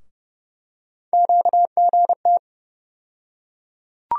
Morse code, heard as a keyed tone, about 30 wpm, 700 Hz. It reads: QGT
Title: USA Call Sign Postfixes - 3 Letters 30wpm